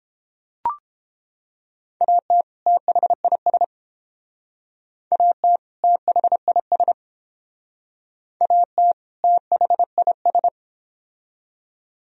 Morse code, heard as a keyed tone, about 28 wpm, 700 Hz. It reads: AT THIS AT THIS AT THIS